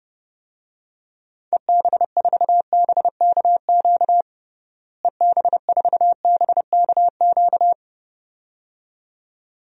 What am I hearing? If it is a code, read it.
Morse code, heard as a keyed tone, about 29 wpm, 700 Hz. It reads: EB4BKQ EB4BKQ
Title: DXCC-100 Call Signs - 6 Characters - Repeated 2x - 30wpm